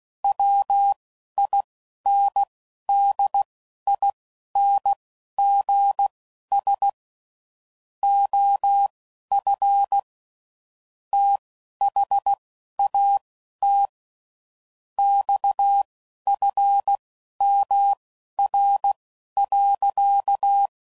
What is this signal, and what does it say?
Morse code, heard as a keyed tone, about 16 wpm, spaced as for 11 wpm, 800 Hz. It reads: WINDINGS OF THAT XFMR.
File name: CW Code Groups @10WPM / Technical Subjects / SwPwrSup